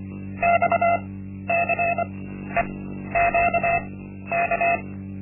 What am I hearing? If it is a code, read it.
Morse code, heard as a keyed tone, about 25 wpm, spaced as for 12 wpm, 690 Hz. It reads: XCEQK
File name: Linksmakalnis-1